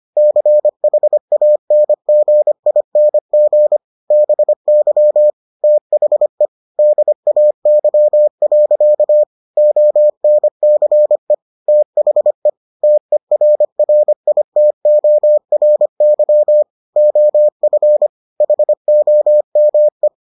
Morse code, heard as a keyed tone, about 25 wpm, 600 Hz. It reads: CHANGING BY THE DAY. ONCE THE TERRITORY OF HOME